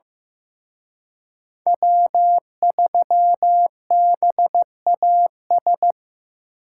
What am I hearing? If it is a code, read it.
Morse code, heard as a keyed tone, about 18 wpm, 700 Hz. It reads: W3BAS
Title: Speed Racing - USA 1x3 Call Signs 15wpm